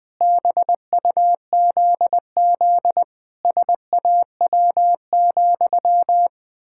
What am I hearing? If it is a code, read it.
Morse code, heard as a keyed tone, about 20 wpm, 700 Hz. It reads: BUZZ SAW,
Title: Pangram Challenge v2 - 20wpm